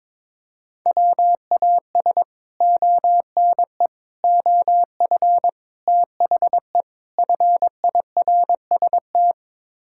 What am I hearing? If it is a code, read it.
Morse code, heard as a keyed tone, about 21 wpm, 700 Hz. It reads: WAS ONE OF THE FIRST
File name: practice / 22wpm / Build A Phrase - 5-Word Phrases - 22wpm